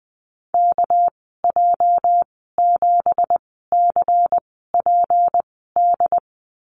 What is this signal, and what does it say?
Morse code, heard as a keyed tone, about 20 wpm, spaced as for 14 wpm, 700 Hz. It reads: KJ7CPD
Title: 231 Lesson 37g - Letter x - Call Signs 20wpm